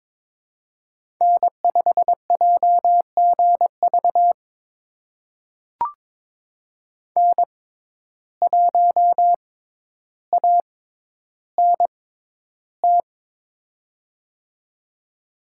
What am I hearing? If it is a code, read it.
Morse code, heard as a keyed tone, about 22 wpm, spaced as for 11 wpm, 700 Hz. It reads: N5JGV N1ANT
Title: USA Call Signs - 6x Character Spacing - 22wpm